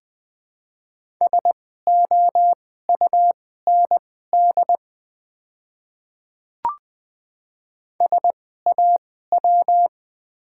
Morse code, heard as a keyed tone, about 20 wpm, spaced as for 15 wpm, 700 Hz. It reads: SOUND SAW
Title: 2x Character Spacing - Top 300 Words - 20wpm